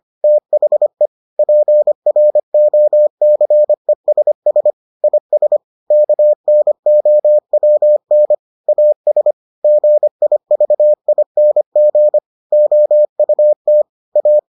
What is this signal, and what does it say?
Morse code, heard as a keyed tone, about 25 wpm, 600 Hz. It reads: THE PROCESS IS KNOWN AS GIVING OUT A